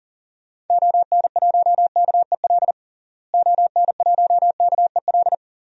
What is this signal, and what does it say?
Morse code, heard as a keyed tone, about 40 wpm, 700 Hz. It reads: ON1KEL ON1KEL
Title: DXCC-100 Call Signs - 6 Characters - Repeated 3x - 40wpm